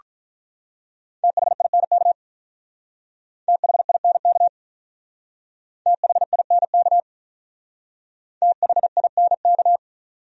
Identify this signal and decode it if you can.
Morse code, heard as a keyed tone, about 42 wpm, 700 Hz. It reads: THINK THINK THINK THINK
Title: Speed-Racing Top 200 Words 35wpm